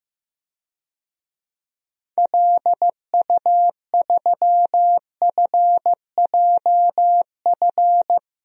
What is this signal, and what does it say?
Morse code, heard as a keyed tone, about 16 wpm, 700 Hz. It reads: LU3FJF
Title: DXCC-100 Call Signs - 6 Characters - Repeated 2x - 15wpm